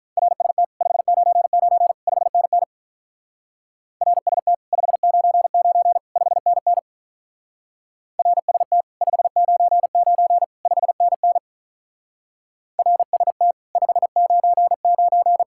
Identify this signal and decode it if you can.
Morse code, heard as a keyed tone, about 44 wpm, 700 Hz. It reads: RST 599 5NN RST 599 5NN RST 599 5NN RST 599
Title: Speed-Racing Common QSO Phrases - 35wpm